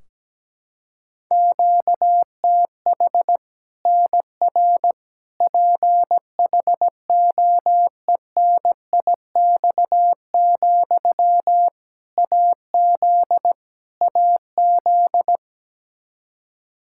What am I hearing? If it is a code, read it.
Morse code, heard as a keyed tone, about 17 wpm, 700 Hz. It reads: QTH NR PHOENIX, AZ AZ
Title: K1USN SST QSO Practice 17wpm